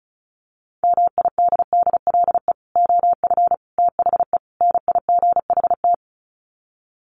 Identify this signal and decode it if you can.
Morse code, heard as a keyed tone, about 34 wpm, 700 Hz. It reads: MIDDLE OF THE NIGHT
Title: Sets of 4 Words 35wpm